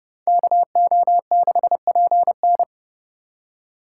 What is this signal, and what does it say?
Morse code, heard as a keyed tone, about 29 wpm, 700 Hz. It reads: KO6PN